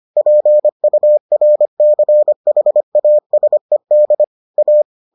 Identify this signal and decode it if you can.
Morse code, heard as a keyed tone, about 25 wpm, 600 Hz. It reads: PURCHASED A